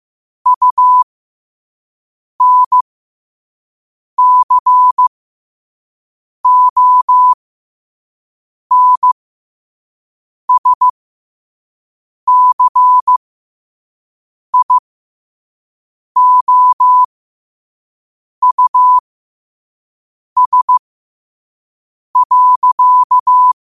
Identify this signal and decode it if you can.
Morse code, heard as a keyed tone, about 15 wpm, spaced as for 5 wpm, 1000 Hz. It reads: UNCONSCIOUS.
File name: qotdfiles / 5wpm